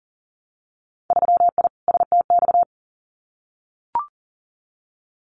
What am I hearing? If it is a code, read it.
Morse code, heard as a keyed tone, about 43 wpm, 700 Hz. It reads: 3I STX
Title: Winter Field Day Exchanges Encore 40wpm